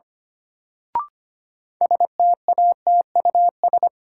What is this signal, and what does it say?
Morse code, heard as a keyed tone, about 25 wpm, 700 Hz. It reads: STATUS
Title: Sentences from Top 1400 Words - New Word Drill - 25wpm